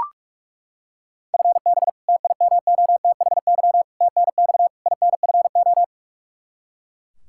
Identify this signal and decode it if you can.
Morse code, heard as a keyed tone, about 41 wpm, 700 Hz. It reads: FB TIMOTHY TNX INFO
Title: Practice QSO - Exchange 3 - More Info - 45wpm